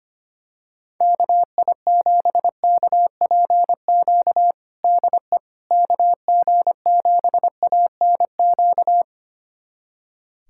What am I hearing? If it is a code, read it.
Morse code, heard as a keyed tone, about 25 wpm, 700 Hz. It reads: KI7KPQ DE KG7ANQ
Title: USA Callsign DE Callsign - 6 Characters - 25wpm